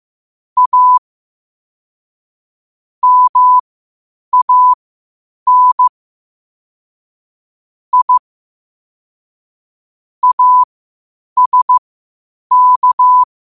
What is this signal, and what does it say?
Morse code, heard as a keyed tone, about 15 wpm, spaced as for 8 wpm, 1000 Hz. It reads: A MAN I ASK
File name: qotdfiles / 8wpm